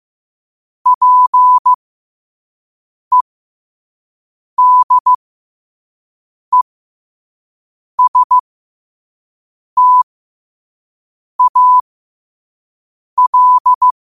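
Morse code, heard as a keyed tone, about 15 wpm, spaced as for 5 wpm, 1000 Hz. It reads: PEDESTAL
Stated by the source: Morse sent at 5 wpm.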